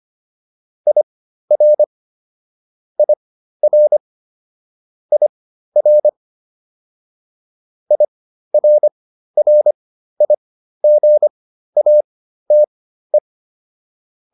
Morse code, heard as a keyed tone, about 25 wpm, spaced as for 13 wpm, 600 Hz. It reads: IR IR IR IRRIGATE